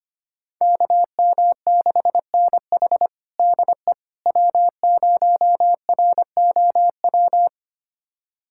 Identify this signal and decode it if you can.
Morse code, heard as a keyed tone, about 25 wpm, 700 Hz. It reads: KM6NH DE W0ROW
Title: USA Callsign DE Callsign - 5 Characters - 25wpm